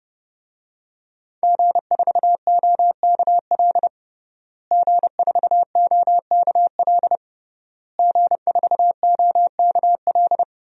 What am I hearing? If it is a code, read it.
Morse code, heard as a keyed tone, about 30 wpm, 700 Hz. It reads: G4OKL G4OKL G4OKL
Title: DXCC-100 Call Signs - 5 Characters - Repeated 3x - 30wpm